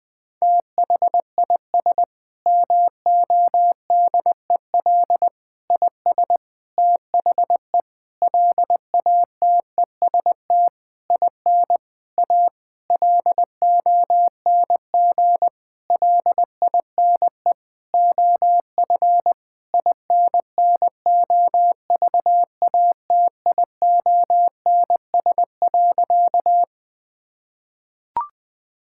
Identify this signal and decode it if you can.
Morse code, heard as a keyed tone, about 20 wpm, 700 Hz. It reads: THIS MODEL IS THE LATEST IN A LONG LINE OF INNOVATIONS.
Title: Sentences with 7-Word Phrases - Spoken Beginning - 20wpm